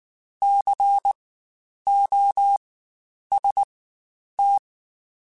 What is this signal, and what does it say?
Morse code, heard as a keyed tone, about 19 wpm, spaced as for 9 wpm, 800 Hz. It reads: COST
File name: Spaced Alphas / Spaced Alphas mp3s / Cs sp @18